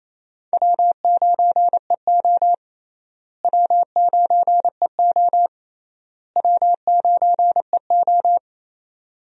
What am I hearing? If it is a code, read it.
Morse code, heard as a keyed tone, about 28 wpm, 700 Hz. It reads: W9EO W9EO W9EO
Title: USA Call Signs - 4 Characters - Repeated 3x - 28wpm